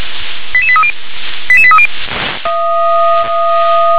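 Five-tone selective call (ZVEI), heard as quick runs of five tones, tones of about 70 ms. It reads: 88830 88830
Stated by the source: ZVEI five-tone selective call received on SDR.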